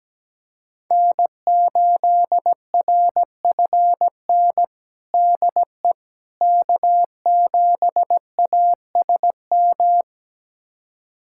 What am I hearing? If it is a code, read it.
Morse code, heard as a keyed tone, about 17 wpm, 700 Hz. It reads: N8RFN DE K7ASM
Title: USA Callsign DE Callsign - 5 Characters - 17wpm